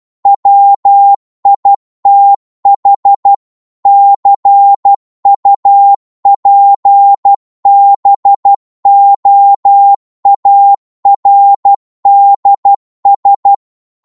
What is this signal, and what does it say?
Morse code, heard as a keyed tone, about 12 wpm, 800 Hz. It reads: WITH CUPBOARDS
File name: cwmp3 / psg-1-12wpm